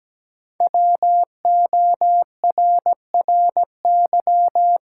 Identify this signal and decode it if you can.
Morse code, heard as a keyed tone, about 17 wpm, 700 Hz. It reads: WORRY
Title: Sentences from Top 1100 Words - New Word Drill - 17wpm